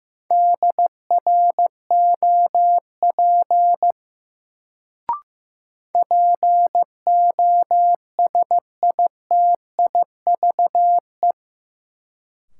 Morse code, heard as a keyed tone, about 15 wpm, 700 Hz. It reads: DROP POSITIVE
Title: Sentences from Top 1400 Words - New Word Drill Alt - 15wpm